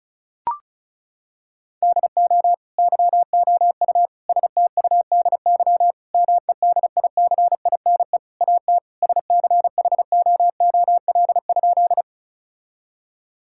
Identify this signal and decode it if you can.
Morse code, heard as a keyed tone, about 36 wpm, 700 Hz. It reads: DO YOU STUDY MEDICINE AT SCHOOL?
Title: Sentences from Top 1000 Words - Review Part 6 - 35wpm